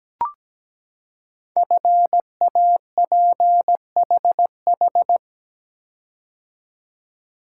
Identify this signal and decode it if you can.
Morse code, heard as a keyed tone, about 17 wpm, 700 Hz. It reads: FAPHH
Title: Code Groups - 5 Letters - 17wpm